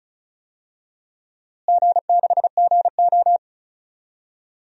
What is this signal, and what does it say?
Morse code, heard as a keyed tone, about 36 wpm, 700 Hz. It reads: G6GO
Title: DXCC-100 Call Signs - 4 Characters - Repeated 2x - 35wpm